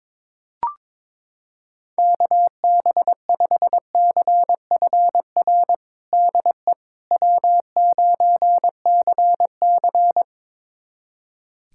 Morse code, heard as a keyed tone, about 22 wpm, 700 Hz. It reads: KB5CFR DE W9CC
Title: USA Callsign DE Callsign 22wpm